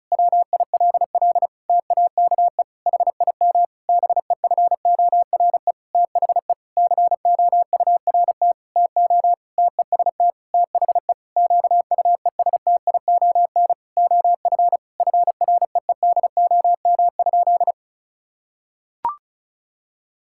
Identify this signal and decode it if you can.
Morse code, heard as a keyed tone, about 36 wpm, 700 Hz. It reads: WILL TAKE HIM BEFORE THE COURT TO TEST THE QUESTION OF FREEDOM?
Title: Sentences from Top 1600 Words 35wpm